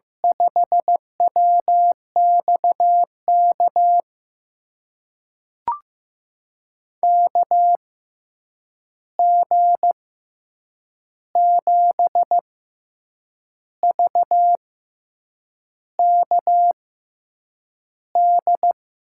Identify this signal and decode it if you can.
Morse code, heard as a keyed tone, about 15 wpm, spaced as for 7 wpm, 700 Hz. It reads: 5WXK KG7VKD
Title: USA Call Signs - 6 Characters - 6x Character Spacing - 15wpm